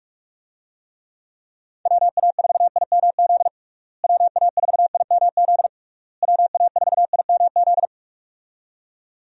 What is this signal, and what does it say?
Morse code, heard as a keyed tone, about 46 wpm, 700 Hz. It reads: WA4IMZ WA4IMZ WA4IMZ
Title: USA Call Signs - 6 Characters - Repeated 3x - 45wpm